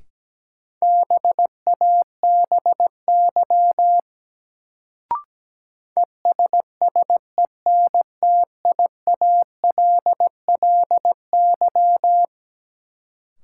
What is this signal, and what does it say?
Morse code, heard as a keyed tone, about 17 wpm, 700 Hz. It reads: BABY ESSENTIALLY
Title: Sentences from Top 1300 Words - New Word Drill - 17wpm